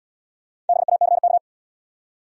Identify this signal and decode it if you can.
Morse code, heard as a keyed tone, about 69 wpm, 700 Hz. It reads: BACK